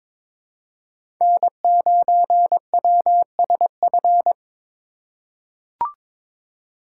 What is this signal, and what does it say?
Morse code, heard as a keyed tone, about 22 wpm, 700 Hz. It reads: N9WSF
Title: USA Call Signs - 5 Characters - 2x Character Spacing - 22wpm